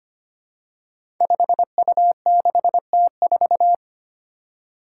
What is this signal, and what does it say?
Morse code, heard as a keyed tone, about 25 wpm, 700 Hz. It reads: 5U6T4
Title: Five Challenging Letter-Number Combinations 25wpm